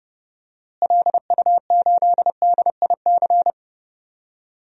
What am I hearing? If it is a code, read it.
Morse code, heard as a keyed tone, about 30 wpm, 700 Hz. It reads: LU8DIC
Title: DXCC-100 Call Signs - 6 Characters - Repeated 2x - 30wpm